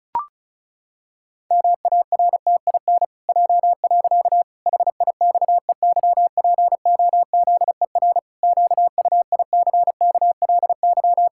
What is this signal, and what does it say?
Morse code, heard as a keyed tone, about 36 wpm, 700 Hz. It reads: MARTIN J. HIXEYPOZER QUICKLY
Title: Pangram Challenge v2 - 35wpm